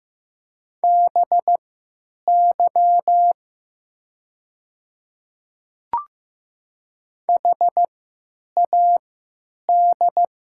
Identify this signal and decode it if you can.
Morse code, heard as a keyed tone, about 15 wpm, spaced as for 9 wpm, 700 Hz. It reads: BY HAD
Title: 3x Character Spacing - Top 100 Words - 15wpm